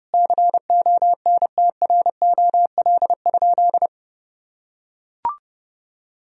Morse code, heard as a keyed tone, about 30 wpm, 700 Hz. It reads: CONTROL?
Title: Sentences from Top 1000 Words - Review Part 3 - 30wpm